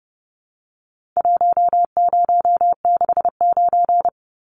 Morse code, heard as a keyed tone, about 30 wpm, 700 Hz. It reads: 1069
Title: Numbers - Contest Style - 4-Digit 30wpm